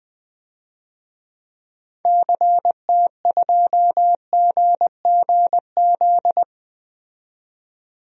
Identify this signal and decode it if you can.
Morse code, heard as a keyed tone, about 20 wpm, 700 Hz. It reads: CT2GGZ
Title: DXCC-100 Call Signs - 6 Characters - 20wpm